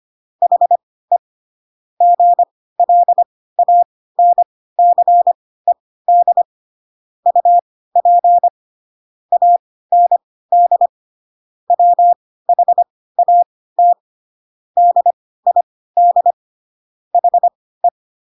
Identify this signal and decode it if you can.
Morse code, heard as a keyed tone, about 25 wpm, spaced as for 16 wpm, 700 Hz. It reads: HE GLANCED UP AND WHAT DID HE
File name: Pinocchio / Pinocchio ch3 10 25 ik1hge